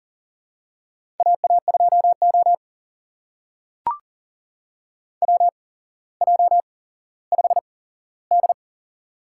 Morse code, heard as a keyed tone, about 40 wpm, spaced as for 17 wpm, 700 Hz. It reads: AA2O WJ5D
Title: USA Call Signs - 8x Character Spacing - 40wpm